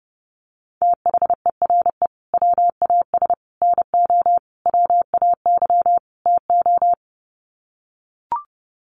Morse code, heard as a keyed tone, about 30 wpm, 700 Hz. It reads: THERE WAS NO WAY TO
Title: Sets of 5 Words Encore 30wpm